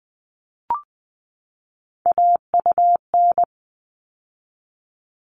Morse code, heard as a keyed tone, about 20 wpm, 700 Hz. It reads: AUN
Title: All Cut Numbers - 3-Digit 20wpm